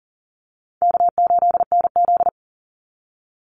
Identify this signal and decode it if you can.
Morse code, heard as a keyed tone, about 40 wpm, 700 Hz. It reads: K8NZ
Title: Easy Call Signs Repeated 2x - 40wpm